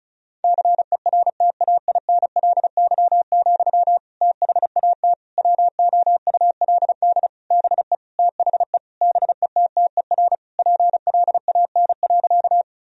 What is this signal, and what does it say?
Morse code, heard as a keyed tone, about 36 wpm, 700 Hz. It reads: CERTAINLY, THAT WOULD BE THE BETTER PLAN.
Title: Sentences from Top 1000 Words - Review Part 2 - 35wpm